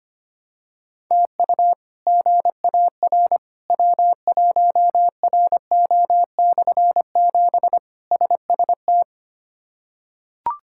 Morse code, heard as a keyed tone, about 25 wpm, 700 Hz. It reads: TU GAR W1RO/7 SST
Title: K1USN SST QSO Practice - 25wpm